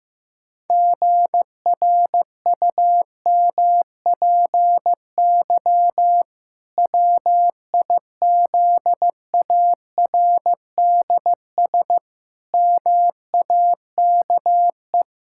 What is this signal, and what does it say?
Morse code, heard as a keyed tone, about 16 wpm, 700 Hz. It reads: GRUMPY WIZARDS MAKE
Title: Pangram Challenge v2 - 15wpm